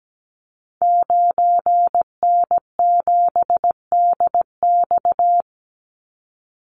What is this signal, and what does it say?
Morse code, heard as a keyed tone, about 17 wpm, 700 Hz. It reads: 9N7DX
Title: Call Signs Repeated 2x - 17wpm